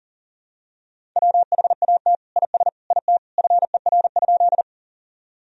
Morse code, heard as a keyed tone, about 40 wpm, 700 Hz. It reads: WHAT IS IT FER?
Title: Sentences from Top 100 Words and Abbreviations 40wpm